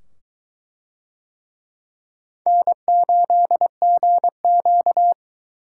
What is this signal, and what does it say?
Morse code, heard as a keyed tone, about 22 wpm, 700 Hz. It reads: N8GQ